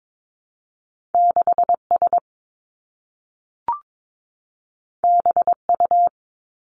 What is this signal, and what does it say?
Morse code, heard as a keyed tone, about 22 wpm, 700 Hz. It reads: 6 S B U
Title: Two Challenging Letter-Number Combinations 22wpm